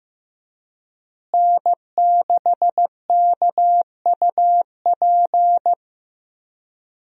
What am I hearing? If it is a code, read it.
Morse code, heard as a keyed tone, about 15 wpm, 700 Hz. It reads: N6KUP
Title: USA Call Signs - 5 Characters - Repeated 3x - 15wpm